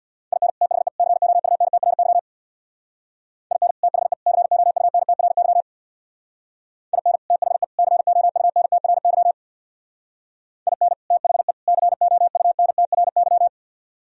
Morse code, heard as a keyed tone, about 62 wpm, 700 Hz. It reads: IN THE COUNTRY IN THE COUNTRY IN THE COUNTRY IN THE COUNTRY
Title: Speed-Racing - Sets of 3 Words 50wpm